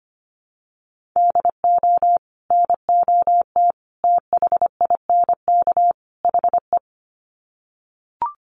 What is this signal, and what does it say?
Morse code, heard as a keyed tone, about 25 wpm, 700 Hz. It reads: DO NOT THINK HE
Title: Sets of 4 Words 25wpm